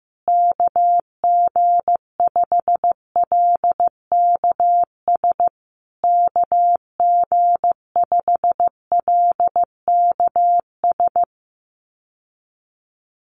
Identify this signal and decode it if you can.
Morse code, heard as a keyed tone, about 15 wpm, 700 Hz. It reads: KG5LKS KG5LKS
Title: USA Call Signs Repeated 3x 15wpm